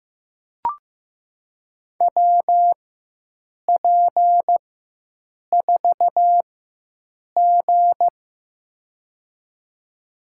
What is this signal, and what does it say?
Morse code, heard as a keyed tone, about 15 wpm, spaced as for 7 wpm, 700 Hz. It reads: WP4G
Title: USA Call Signs - 4 Characters - 4x Character Spacing - 15wpm